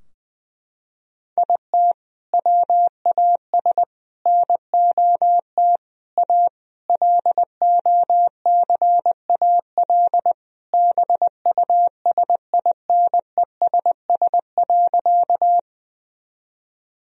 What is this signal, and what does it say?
Morse code, heard as a keyed tone, about 20 wpm, 700 Hz. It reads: IT WAS NOT A LOCAL BUSINESS.
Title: Sentences from Top 1000 Words - Review Part 3 - 20wpm